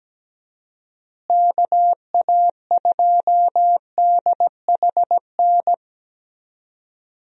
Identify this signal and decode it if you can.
Morse code, heard as a keyed tone, about 17 wpm, 700 Hz. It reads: KA2DHN
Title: USA Call Signs - 6 Characters - Repeated 2x - 17wpm